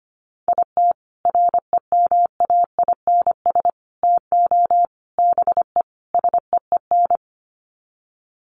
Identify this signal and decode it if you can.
Morse code, heard as a keyed tone, about 25 wpm, 700 Hz. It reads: IT REMAINS TO BE SEEN
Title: Sets of 5 Words Encore 25wpm